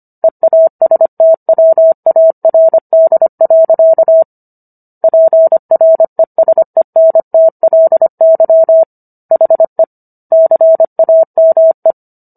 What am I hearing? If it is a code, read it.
Morse code, heard as a keyed tone, about 25 wpm, 650 Hz. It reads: EASTWARD. PRESENTLY HE CAME